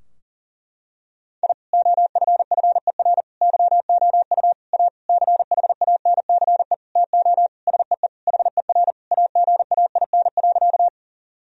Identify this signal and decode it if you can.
Morse code, heard as a keyed tone, about 40 wpm, 700 Hz. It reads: I OFFER YOU A CHANCE TO SEE HER AGAIN.
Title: Sentences from Top 1000 Words - Review Part 4 - 40wpm